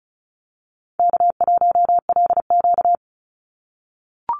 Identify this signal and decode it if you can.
Morse code, heard as a keyed tone, about 34 wpm, 700 Hz. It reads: K1LQ
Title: USA Call Signs 35wpm